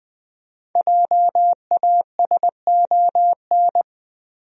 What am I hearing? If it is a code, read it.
Morse code, heard as a keyed tone, about 20 wpm, 700 Hz. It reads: JASON
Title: USA Common Names - Max 5 Letters - 20wpm